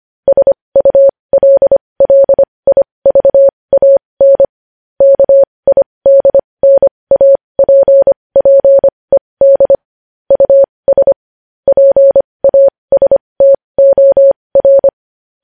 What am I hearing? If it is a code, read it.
Morse code, heard as a keyed tone, about 25 wpm, spaced as for 20 wpm, 550 Hz. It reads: SULLIVAN KIDNAPPED US PASTOR
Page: news-headlines-in-morse-code-at-20-wpm